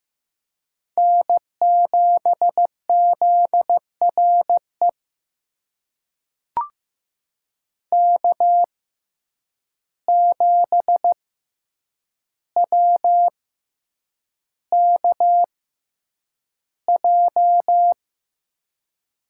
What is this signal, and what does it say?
Morse code, heard as a keyed tone, about 15 wpm, spaced as for 8 wpm, 700 Hz. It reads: N7ZRE K7WKJ
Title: USA Call Signs - 5 Characters - 6x Character Spacing - 15wpm